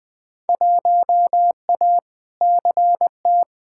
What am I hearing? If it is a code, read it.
Morse code, heard as a keyed tone, about 20 wpm, 700 Hz. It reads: 1A CT
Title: Field Day Exchanges Encore - 20wpm